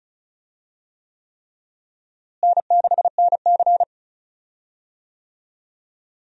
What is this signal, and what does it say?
Morse code, heard as a keyed tone, about 36 wpm, 700 Hz. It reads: N6NC
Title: USA Call Signs - 4 Characters - Repeated 3x - 35wpm